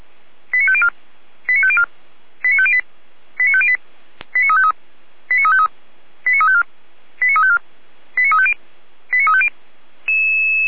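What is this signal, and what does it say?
Five-tone selective call (ZVEI), heard as quick runs of five tones, tones of about 70 ms. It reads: 89504 89504 89508 89508 89353 89353 89354 89354 89350 89350
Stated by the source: SDR-received ZVEI five-tone selective call logged by audio.f85360.feuerwehr-orlamuende.org, fire brigade Orlamünde.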